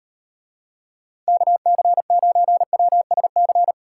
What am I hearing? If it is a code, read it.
Morse code, heard as a keyed tone, about 35 wpm, 700 Hz. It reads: KC9WSC